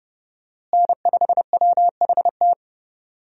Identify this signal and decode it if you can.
Morse code, heard as a keyed tone, about 31 wpm, 700 Hz. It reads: N5WHT
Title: USA Call Signs - 5 Characters - Repeated 2x - 30wpm